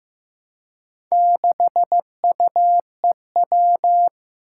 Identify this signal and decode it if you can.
Morse code, heard as a keyed tone, about 15 wpm, 700 Hz. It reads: 6UEW
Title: Four Challenging Letter-Number Combinations 15wpm